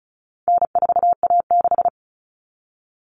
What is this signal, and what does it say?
Morse code, heard as a keyed tone, about 34 wpm, 700 Hz. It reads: N4A6
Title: Common Cut Numbers - 4-Digit 35wpm